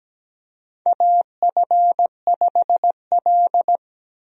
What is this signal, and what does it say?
Morse code, heard as a keyed tone, about 17 wpm, 700 Hz. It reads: AF5L
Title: USA Call Signs - 4 Characters - 17wpm